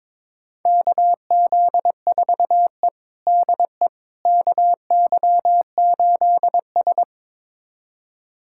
Morse code, heard as a keyed tone, about 22 wpm, 700 Hz. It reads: KZ4E DE KY8S
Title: USA Callsign DE Callsign - 4 Characters - 22wpm